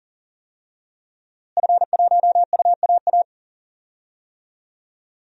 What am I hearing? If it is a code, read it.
Morse code, heard as a keyed tone, about 40 wpm, 700 Hz. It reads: F1UAA
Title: DXCC-100 Call Signs - 5 Characters - Repeated 3x - 40wpm